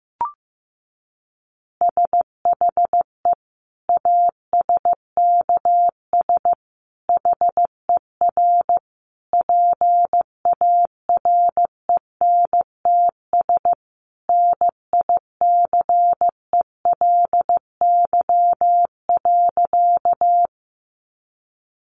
Morse code, heard as a keyed tone, about 14 wpm, 700 Hz. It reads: SHE ASKS HER PARENTS NICELY.
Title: Easy Phrases 15wpm